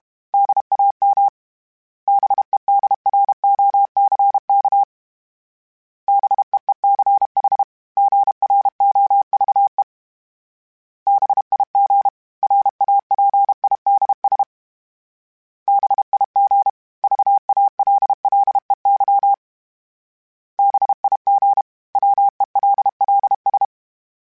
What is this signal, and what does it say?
Morse code, heard as a keyed tone, about 32 wpm, 800 Hz. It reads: DAM BEDROCK BEECH GROVE BIG RAPIDS BIG VALLEY BIG WELLS
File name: fromSMs / FromSMs mp3s / Cities@30